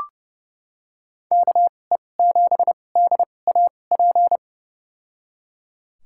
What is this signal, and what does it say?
Morse code, heard as a keyed tone, about 30 wpm, spaced as for 22 wpm, 700 Hz. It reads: KE7DAP
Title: USA Call Signs - 6 Characters - 2x Character Spacing - 30wpm